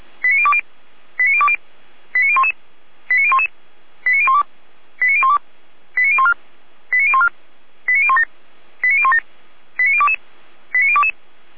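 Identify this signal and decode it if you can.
Five-tone selective call (ZVEI), heard as quick runs of five tones, tones of about 70 ms. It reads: 89020 89020 89011 89011 89012 89012 89014 89014 89017 89017 89022 89022
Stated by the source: ZVEI five-tone selective call received by SDR, from the Feuerwehr Orlamünde fire brigade audio server.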